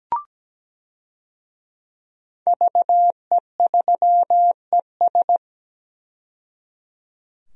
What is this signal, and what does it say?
Morse code, heard as a keyed tone, about 17 wpm, 700 Hz. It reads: VE3ES
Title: DXCC-100 Call Signs - 5 Characters - 17wpm